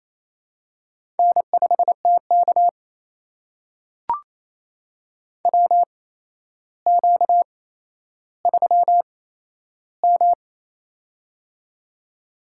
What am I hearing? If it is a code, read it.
Morse code, heard as a keyed tone, about 28 wpm, spaced as for 12 wpm, 700 Hz. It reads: N5TK WQ3M
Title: USA Call Signs - 4 Characters - 8x Character Spacing - 28wpm